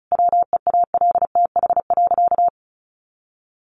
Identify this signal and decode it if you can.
Morse code, heard as a keyed tone, about 34 wpm, 700 Hz. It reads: WEALTH.
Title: Sentences from Top 1200 Words 35wpm